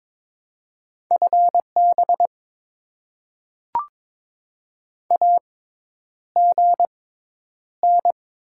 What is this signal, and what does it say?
Morse code, heard as a keyed tone, about 22 wpm, spaced as for 10 wpm, 690 Hz. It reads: FB AGN
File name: Top 20 QSO Elements - 6x Character Spacing - 22wpm